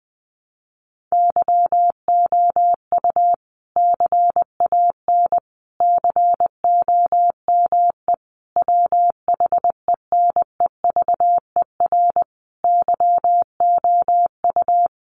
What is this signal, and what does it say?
Morse code, heard as a keyed tone, about 20 wpm, 700 Hz. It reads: YOU CAN COME WHENEVER YOU
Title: Sentences from Top 1200 Words 20wpm